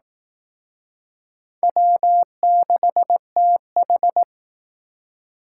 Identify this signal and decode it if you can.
Morse code, heard as a keyed tone, about 18 wpm, 700 Hz. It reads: W6TH
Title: Speed Racing - USA 1x2 Call Signs 15wpm